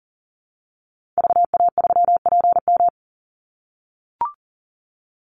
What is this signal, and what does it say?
Morse code, heard as a keyed tone, about 40 wpm, 700 Hz. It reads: VA3PM
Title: Call Signs - 40wpm